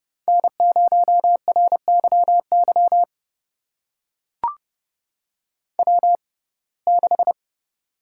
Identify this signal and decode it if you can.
Morse code, heard as a keyed tone, about 30 wpm, spaced as for 22 wpm, 700 Hz. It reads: N0RYY W6
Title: USA Call Signs - 5 Characters - 6x Character Spacing - 30wpm